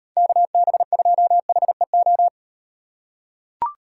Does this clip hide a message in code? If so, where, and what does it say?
Yes: Morse code from 0.2 to 2.3 s, heard as a keyed tone, about 38 wpm, 700 Hz. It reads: KB2HEO